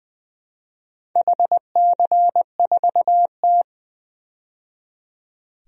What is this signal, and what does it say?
Morse code, heard as a keyed tone, about 20 wpm, 700 Hz. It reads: HC4T
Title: DXCC-100 Call Signs - 4 Characters - 20wpm